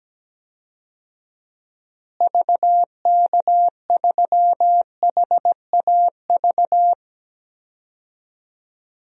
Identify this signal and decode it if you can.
Morse code, heard as a keyed tone, about 17 wpm, 700 Hz. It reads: VK3HAV
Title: DXCC-100 Call Signs - 6 Characters - Repeated 2x - 17wpm